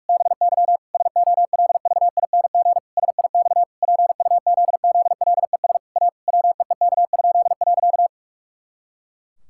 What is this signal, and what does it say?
Morse code, heard as a keyed tone, about 46 wpm, 700 Hz. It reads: BY SOLVING SIX PUZZLES A WEEK?.
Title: Pangram Challenge v2 - 45wpm